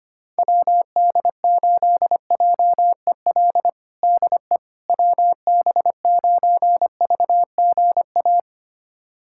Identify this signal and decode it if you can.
Morse code, heard as a keyed tone, about 25 wpm, 700 Hz. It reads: WD8JEL DE WB9VGA
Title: USA Callsign DE Callsign - 6 Characters - 25wpm